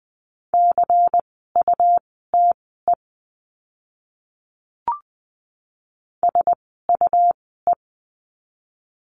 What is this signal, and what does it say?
Morse code, heard as a keyed tone, about 20 wpm, spaced as for 15 wpm, 700 Hz. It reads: CUTE SUE
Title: 101 Lesson 17f - Letter u - Words 20wpm